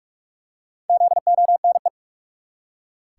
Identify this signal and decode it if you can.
Morse code, heard as a keyed tone, about 44 wpm, 700 Hz. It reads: ZONE
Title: Four Letter Words - 45wpm